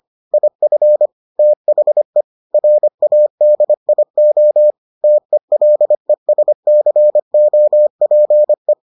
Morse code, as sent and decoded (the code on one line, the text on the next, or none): .. ..-. / - .... / .-. .- -.. .. --- / - . .-.. ... -.-. --- .--. .
IF THE RADIO TELESCOPE